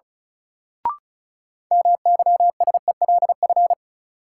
Morse code, heard as a keyed tone, about 36 wpm, 700 Hz. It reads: MYSELF
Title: Sentences from Top 1300 Words - New Word Drill - 35wpm